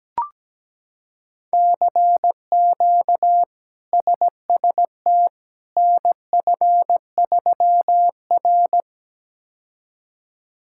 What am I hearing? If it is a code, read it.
Morse code, heard as a keyed tone, about 17 wpm, 700 Hz. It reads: CQ SST NF3R
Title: K1USN SST QSO Practice - 17wpm